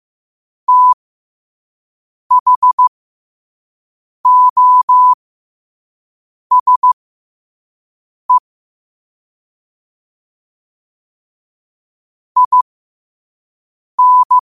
Morse code, heard as a keyed tone, about 15 wpm, spaced as for 5 wpm, 1000 Hz. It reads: THOSE IN
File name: qotdfiles / 5wpm